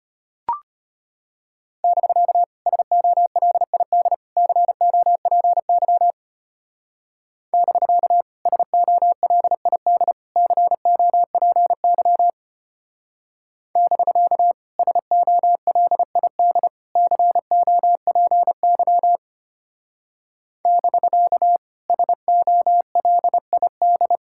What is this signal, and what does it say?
Morse code, heard as a keyed tone, about 32 wpm, 700 Hz. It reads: <BK> SOLID COPY <BK> SOLID COPY <BK> SOLID COPY <BK> SOLID